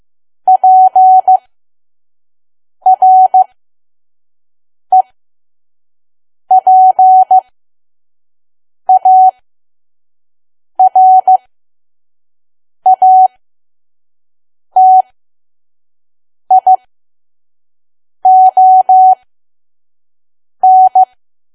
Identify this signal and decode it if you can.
Morse code, heard as a keyed tone, about 15 wpm, spaced as for 5 wpm, 750 Hz. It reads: PREPARATION
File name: Archive / 5 WPM files / PM